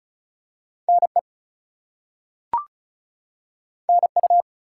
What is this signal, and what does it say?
Morse code, heard as a keyed tone, about 36 wpm, 700 Hz. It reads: NE NU
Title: US States and Canadian Provinces 35wpm